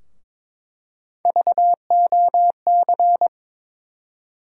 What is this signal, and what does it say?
Morse code, heard as a keyed tone, about 22 wpm, 700 Hz. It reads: VOC
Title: USA Call Sign Postfixes - 3 Letters 22wpm